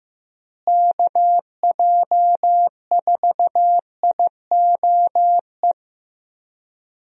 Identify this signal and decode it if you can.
Morse code, heard as a keyed tone, about 15 wpm, 700 Hz. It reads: KJ4IOE
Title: USA Call Signs - 6 Characters - Repeated 2x - 15wpm